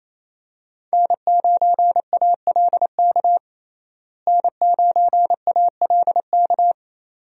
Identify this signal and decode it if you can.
Morse code, heard as a keyed tone, about 28 wpm, 700 Hz. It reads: N9ALK N9ALK
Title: USA Call Signs - 5 Characters - Repeated 3x - 28wpm